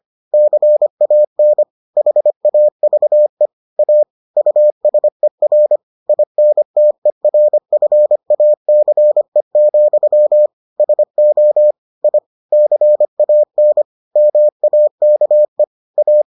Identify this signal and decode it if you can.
Morse code, heard as a keyed tone, about 25 wpm, 600 Hz. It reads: CAN HAVE A USER INTERFACE, SO I CAN MAKE A